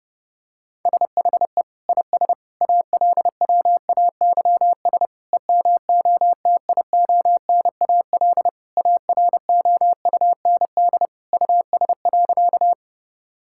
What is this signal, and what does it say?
Morse code, heard as a keyed tone, about 30 wpm, 700 Hz. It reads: SHE IS ALWAYS EMOTIONAL AROUND US.
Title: Sentences from Top 1000 Words - Review Part 5 - 30wpm